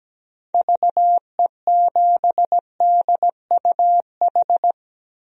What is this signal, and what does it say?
Morse code, heard as a keyed tone, about 17 wpm, 700 Hz. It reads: VE7DUH
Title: DXCC-100 Call Signs - 6 Characters - 17wpm